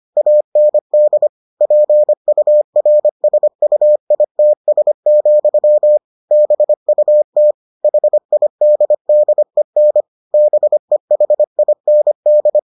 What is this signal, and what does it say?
Morse code, heard as a keyed tone, about 25 wpm, 600 Hz. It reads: AND PURSUITS, BUT HIDDEN BEHIND